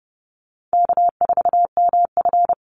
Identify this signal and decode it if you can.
Morse code, heard as a keyed tone, about 29 wpm, 700 Hz. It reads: K4MF
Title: NA SOTA Chaser Callsigns 30wpm